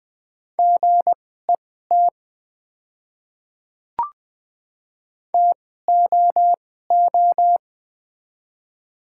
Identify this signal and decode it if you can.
Morse code, heard as a keyed tone, about 20 wpm, spaced as for 15 wpm, 700 Hz. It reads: GET TOO
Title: 2x Character Spacing - Top 300 Words - 20wpm